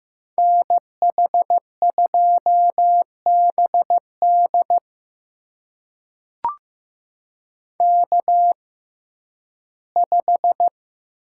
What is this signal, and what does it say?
Morse code, heard as a keyed tone, about 15 wpm, spaced as for 11 wpm, 700 Hz. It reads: NH2BD K5
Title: USA Call Signs - 6x Character Spacing - 15wpm